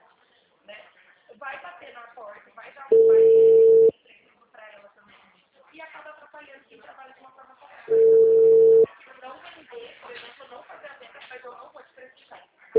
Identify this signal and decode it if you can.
Ringback tone (China, 450 Hz), with speech.